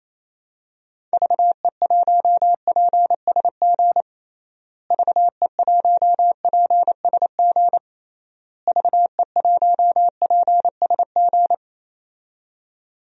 Morse code, heard as a keyed tone, about 28 wpm, 700 Hz. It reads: VE1PSG VE1PSG VE1PSG
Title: DXCC-100 Call Signs - 6 Characters - Repeated 3x - 28wpm